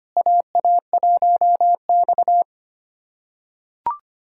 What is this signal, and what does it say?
Morse code, heard as a keyed tone, about 25 wpm, 700 Hz. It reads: AA1X
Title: USA Callsign DE Callsign - 4 Characters - 25wpm